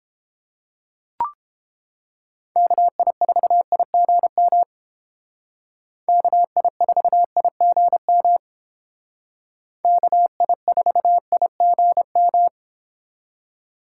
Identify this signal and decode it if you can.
Morse code, heard as a keyed tone, about 30 wpm, 700 Hz. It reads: KI4IGM KI4IGM KI4IGM